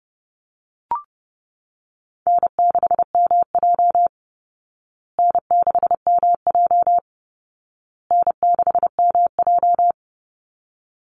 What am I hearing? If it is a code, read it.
Morse code, heard as a keyed tone, about 30 wpm, 700 Hz. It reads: N6MJ N6MJ N6MJ
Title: Easy Call Signs Repeated 3x - 30wpm